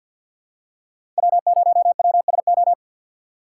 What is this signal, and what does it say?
Morse code, heard as a keyed tone, about 53 wpm, 700 Hz. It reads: W0WSO
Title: USA Call Signs - 5 Characters - 6x Character Spacing - 50wpm